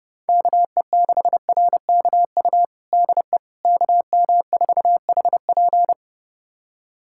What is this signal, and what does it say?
Morse code, heard as a keyed tone, about 30 wpm, 700 Hz. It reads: KE6RKU DE KM4HP
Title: USA Callsign DE Callsign 30wpm